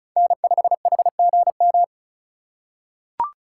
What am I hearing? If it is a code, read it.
Morse code, heard as a keyed tone, about 34 wpm, 700 Hz. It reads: N5HGM
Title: USA Call Signs - 5 Characters - 4x Character Spacing - 35wpm